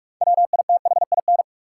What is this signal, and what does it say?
Morse code, heard as a keyed tone, about 47 wpm, 700 Hz. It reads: WITHIN
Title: Sentences from Top 700 Words - New Word Drill - 45wpm